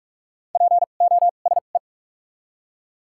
Morse code, heard as a keyed tone, about 45 wpm, spaced as for 32 wpm, 700 Hz. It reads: POSE
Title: 2x Character Spacing - Top 500 Words - 45wpm